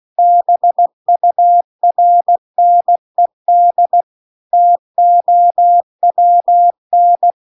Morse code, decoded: BURNED TOWN

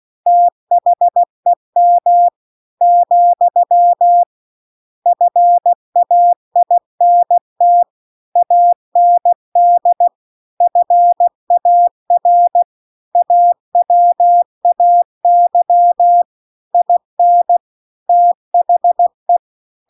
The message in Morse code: THEM , FAINT AND FAR AWAY IN THE

